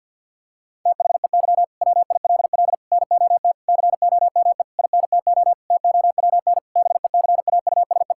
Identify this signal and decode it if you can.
Morse code, heard as a keyed tone, about 51 wpm, 700 Hz. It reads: THEY WILL NOT COME INTO TOWN BECAUSE
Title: Sentences from Top 1000 Words - Review Part 5 - 50wpm